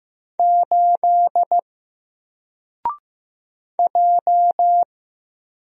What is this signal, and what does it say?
Morse code, heard as a keyed tone, about 15 wpm, 700 Hz. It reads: 8 J
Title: Single Character (Letter:Number:Punctuation:Prosign) 15wpm